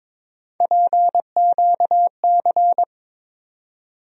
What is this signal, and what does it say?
Morse code, heard as a keyed tone, about 22 wpm, 700 Hz. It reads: PQC
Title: USA Call Sign Postfixes - 3 Letters 22wpm